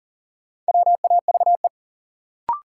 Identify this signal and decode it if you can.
Morse code, heard as a keyed tone, about 38 wpm, 700 Hz. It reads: WAVE
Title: Sentences from Top 1400 Words - New Word Drill - 40wpm